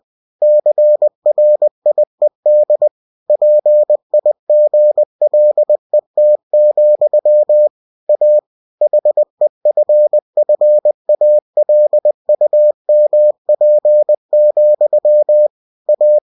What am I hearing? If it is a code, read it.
Morse code, heard as a keyed tone, about 20 wpm, 600 Hz. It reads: CRIED PIGLET, A HEFFALUMP, A